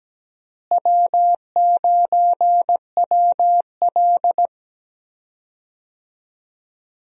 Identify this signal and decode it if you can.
Morse code, heard as a keyed tone, about 17 wpm, 700 Hz. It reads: W9WL
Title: USA Call Signs - 4 Characters - Repeated 3x - 17wpm